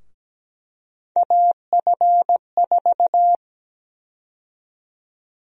Morse code, heard as a keyed tone, about 17 wpm, 700 Hz. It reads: AF4
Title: USA Call Sign Prefixes - Repeated 2x - 17wpm